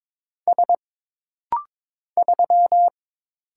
Morse code, heard as a keyed tone, about 22 wpm, 700 Hz. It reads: S 3
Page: This rapid-fire timing is geared to push your ICR (Instant Character Recognition) skills.